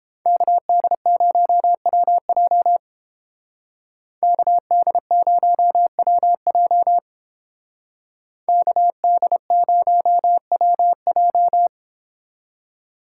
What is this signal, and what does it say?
Morse code, heard as a keyed tone, about 29 wpm, 700 Hz. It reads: KD0WJ KD0WJ KD0WJ